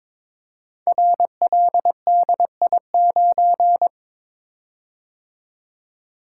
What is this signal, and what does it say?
Morse code, heard as a keyed tone, about 22 wpm, 700 Hz. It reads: RLDI9
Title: Code Groups - 5 Letter:Numbers - 22wpm